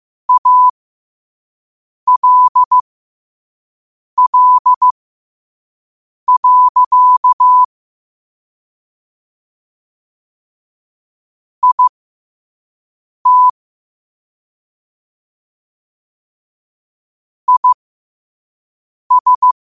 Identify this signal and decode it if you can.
Morse code, heard as a keyed tone, about 15 wpm, spaced as for 5 wpm, 1000 Hz. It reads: ALL. IT IS
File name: qotdfiles / 5wpm